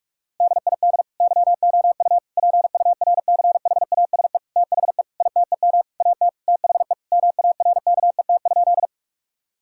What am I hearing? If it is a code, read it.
Morse code, heard as a keyed tone, about 41 wpm, 700 Hz. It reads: DID YOU PURCHASE THE ITEM AT THE MARKET?
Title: Sentences from Top 1000 Words - Review Part 6 - 45wpm